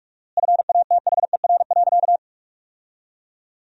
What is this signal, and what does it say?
Morse code, heard as a keyed tone, about 46 wpm, 700 Hz. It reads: FATHER.
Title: 4x Word-Spacing - Sentences from Top 300 Words - 45wpm